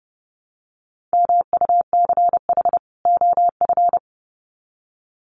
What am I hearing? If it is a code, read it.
Morse code, heard as a keyed tone, about 30 wpm, 700 Hz. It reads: MUCH OF
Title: Sets of 2 Words 30wpm